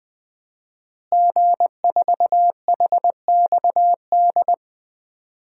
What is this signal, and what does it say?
Morse code, heard as a keyed tone, about 20 wpm, 700 Hz. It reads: G4HXD
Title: DXCC-100 Call Signs - 5 Characters - Repeated 2x - 20wpm